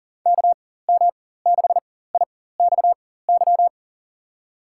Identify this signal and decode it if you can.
Morse code, heard as a keyed tone, about 40 wpm, spaced as for 19 wpm, 700 Hz. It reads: KM6IXY
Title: USA Call Signs - 6 Characters - 4x Character Spacing - 40wpm